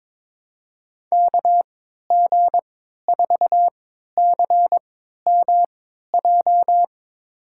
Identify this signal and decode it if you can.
Morse code, heard as a keyed tone, about 21 wpm, spaced as for 12 wpm, 700 Hz. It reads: KG4CMJ
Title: USA Call Signs - 6 Characters - 3x Character Spacing - 22wpm